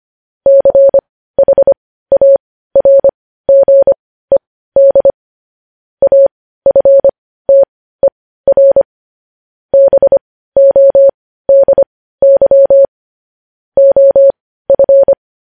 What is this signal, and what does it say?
Morse code, heard as a keyed tone, about 25 wpm, spaced as for 15 wpm, 550 Hz. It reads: CHARGED AFTER BODY OF
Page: news headlines in morse code at 15 wpm